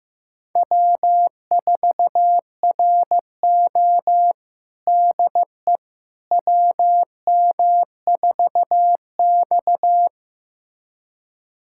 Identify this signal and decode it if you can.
Morse code, heard as a keyed tone, about 15 wpm, 700 Hz. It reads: W4RO DE WM4X
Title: USA Callsign DE Callsign - 4 Characters - 15wpm